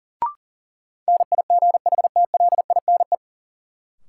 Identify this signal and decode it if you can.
Morse code, heard as a keyed tone, about 40 wpm, 700 Hz. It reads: NIGHTLINE
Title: Sets of 5 Words - New Word Drill - 40wpm